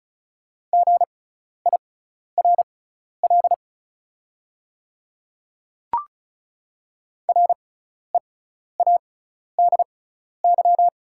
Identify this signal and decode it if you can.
Morse code, heard as a keyed tone, about 36 wpm, spaced as for 12 wpm, 690 Hz. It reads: GIRL READY
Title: 6x Character Spacing - Top 500 Words - 35wpm